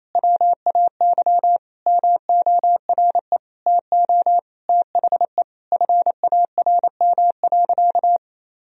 Morse code, heard as a keyed tone, about 28 wpm, 700 Hz. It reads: WAY MORE TO THE FARM.
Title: Sentences from Top 1000 Words - Review Part 2 - 28wpm